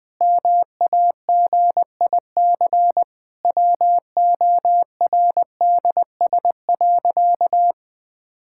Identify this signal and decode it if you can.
Morse code, heard as a keyed tone, about 20 wpm, 700 Hz. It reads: MAGIC WORDS.
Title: Difficult Sentences with Hints - 20wpm